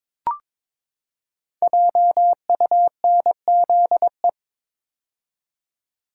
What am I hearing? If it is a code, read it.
Morse code, heard as a keyed tone, about 22 wpm, 700 Hz. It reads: JUNZE
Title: Code Groups - 5 Letters - 22wpm